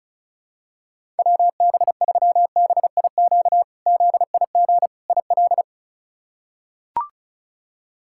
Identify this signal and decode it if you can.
Morse code, heard as a keyed tone, about 36 wpm, 700 Hz. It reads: WB3BIQ ZIG IL
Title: K1USN SST QSO Practice - 35wpm